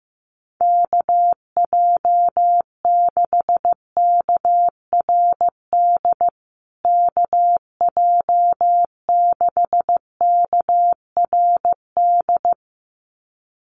Morse code, heard as a keyed tone, about 15 wpm, 700 Hz. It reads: KJ6KRD KJ6KRD
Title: USA Call Signs Repeated 2x 15wpm